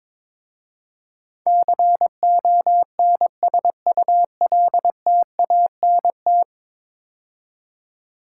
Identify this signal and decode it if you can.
Morse code, heard as a keyed tone, about 22 wpm, 700 Hz. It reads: CONSULTANT
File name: practice / 22wpm / 10 Letter Words - First 3 Letters Given - 22wpm